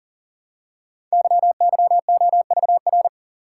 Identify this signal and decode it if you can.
Morse code, heard as a keyed tone, about 42 wpm, 700 Hz. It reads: YYOVR
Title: Code Groups - 5 Letters - 40wpm